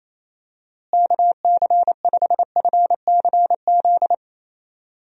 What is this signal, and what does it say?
Morse code, heard as a keyed tone, about 27 wpm, 700 Hz. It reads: KC5FCZ
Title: USA Call Signs - 6 Characters - Repeated 2x - 28wpm